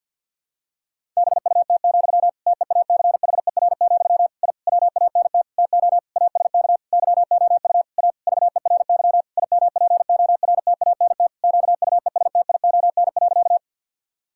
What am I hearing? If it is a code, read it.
Morse code, heard as a keyed tone, about 50 wpm, 700 Hz. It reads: BUT, TEACHER, I WANT TO ASK YOU A VERY IMPORTANT QUESTION.
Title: Sentences from Top 1000 Words - Review Part 6 - 50wpm